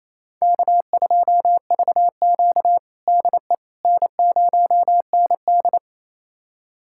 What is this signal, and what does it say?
Morse code, heard as a keyed tone, about 27 wpm, 700 Hz. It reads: K2VQ DE N0ND
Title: USA Callsign DE Callsign - 4 Characters - 28wpm